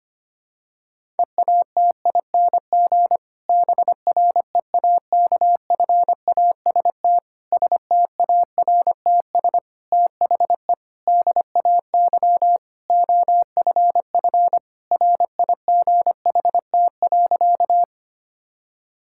Morse code, heard as a keyed tone, about 25 wpm, 700 Hz. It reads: EATING BREAKFAST STARTS THE DAY OFF RIGHT.
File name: practice / 25wpm / Difficult Sentences with Hints - 25wpm